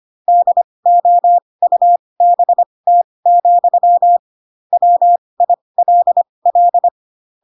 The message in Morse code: DOUBT, WILL

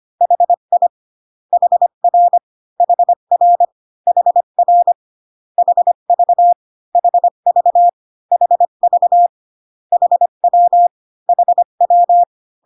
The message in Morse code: HI HR HR HR HV HV HV HW HW

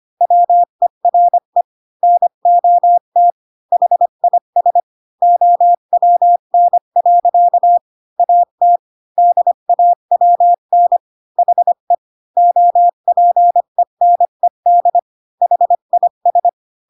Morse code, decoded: WERE NOT HIS OWN. AT DAWN HE OPENED HIS

